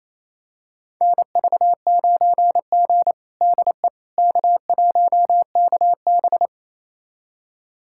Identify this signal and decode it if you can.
Morse code, heard as a keyed tone, about 28 wpm, 700 Hz. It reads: NV9G DE K1KB
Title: USA Callsign DE Callsign - 4 Characters - 28wpm